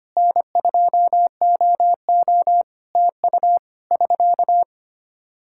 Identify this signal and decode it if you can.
Morse code, heard as a keyed tone, about 25 wpm, 700 Hz. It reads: N2OO TU <SK>
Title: K1USN SST QSO Practice 25wpm